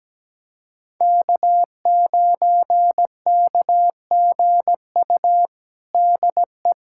Morse code, heard as a keyed tone, about 17 wpm, 700 Hz. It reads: K9KGU DE
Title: USA Callsign DE Callsign - 5 Characters - 17wpm